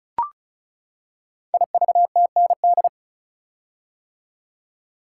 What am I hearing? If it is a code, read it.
Morse code, heard as a keyed tone, about 36 wpm, 700 Hz. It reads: IVTND
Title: Code Groups - 5 Letters - 35wpm